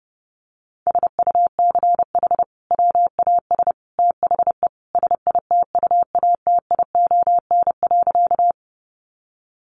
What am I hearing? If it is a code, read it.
Morse code, heard as a keyed tone, about 28 wpm, 700 Hz. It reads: SUCH WAS THE SITUATION.
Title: Sentences from Top 1200 Words 30wpm